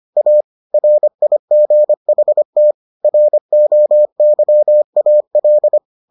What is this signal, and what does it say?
Morse code, heard as a keyed tone, about 25 wpm, 600 Hz. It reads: A RIGHT ROYAL